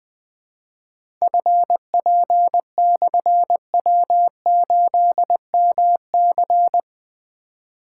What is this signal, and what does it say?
Morse code, heard as a keyed tone, about 20 wpm, 700 Hz. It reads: FP/W8MC
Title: Difficult Call Signs 20wpm